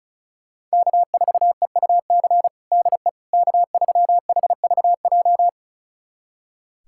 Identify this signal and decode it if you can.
Morse code, heard as a keyed tone, about 36 wpm, 700 Hz. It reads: K4EUC DE K3HVJ
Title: USA Callsign DE Callsign - 5 Characters - 35wpm